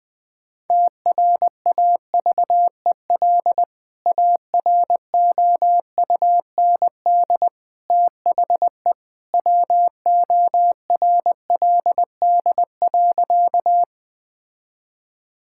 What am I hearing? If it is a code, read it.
Morse code, heard as a keyed tone, about 20 wpm, 700 Hz. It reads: TRAVEL AROUND THE WORLD.
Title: Sentences with 3-Word Phrases - Spoken Beginning - 20wpm